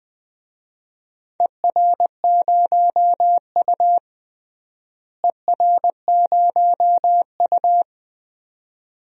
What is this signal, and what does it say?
Morse code, heard as a keyed tone, about 21 wpm, 700 Hz. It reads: ER0U ER0U
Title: DXCC-100 Call Signs - 4 Characters - Repeated 2x - 20wpm